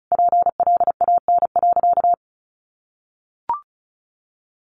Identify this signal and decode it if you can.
Morse code, heard as a keyed tone, about 34 wpm, 700 Hz. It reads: PLAN.
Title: Sentences from Top 500 Words 35wpm